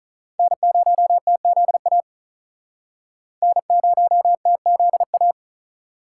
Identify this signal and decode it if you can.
Morse code, heard as a keyed tone, about 45 wpm, 700 Hz. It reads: N0TZA N0TZA